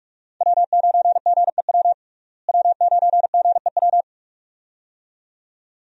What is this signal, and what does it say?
Morse code, heard as a keyed tone, about 46 wpm, 700 Hz. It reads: W9GEW W9GEW
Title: USA Call Signs - 5 Characters - Repeated 2x - 45wpm